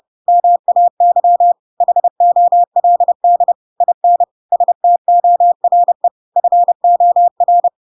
Morse code, heard as a keyed tone, about 30 wpm, 700 Hz. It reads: MAY HOLD IN STORE FOR
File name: Pinocchio / Pinocchio ch7 30 30 ik1hge